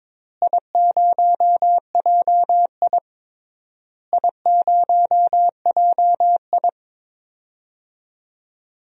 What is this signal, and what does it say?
Morse code, heard as a keyed tone, about 22 wpm, 700 Hz. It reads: I0JI I0JI
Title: DXCC-100 Call Signs - 4 Characters - Repeated 3x - 22wpm